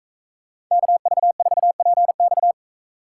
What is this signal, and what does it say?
Morse code, heard as a keyed tone, about 40 wpm, 700 Hz. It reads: KV4PX